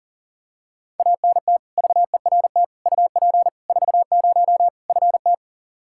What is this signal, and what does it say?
Morse code, heard as a keyed tone, about 40 wpm, 700 Hz. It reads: ANT VERT UP 40 FT
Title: Practice QSO - Exchange 2 - Info - 40wpm